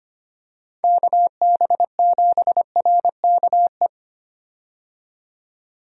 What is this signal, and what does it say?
Morse code, heard as a keyed tone, about 25 wpm, 700 Hz. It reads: KB7RKE
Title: USA Call Signs - 6 Characters - 25wpm